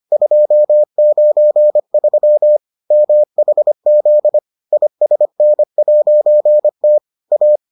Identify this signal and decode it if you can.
Morse code, heard as a keyed tone, about 25 wpm, 600 Hz. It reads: 293 MHZ ISN'T A